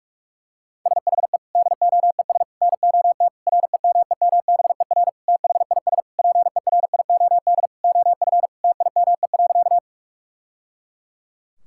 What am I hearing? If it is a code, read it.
Morse code, heard as a keyed tone, about 46 wpm, 700 Hz. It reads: SHE DOES NOT REMEMBER THIS PERIOD OF TIME.
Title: Sentences from Top 1000 Words - Review Part 5 - 45wpm